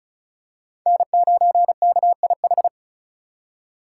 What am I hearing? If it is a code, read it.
Morse code, heard as a keyed tone, about 34 wpm, 700 Hz. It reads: N9KIH
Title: USA Call Signs - 5 Characters - 4x Character Spacing - 35wpm